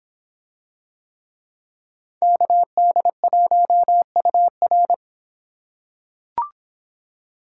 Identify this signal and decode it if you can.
Morse code, heard as a keyed tone, about 26 wpm, 700 Hz. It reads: KD1UR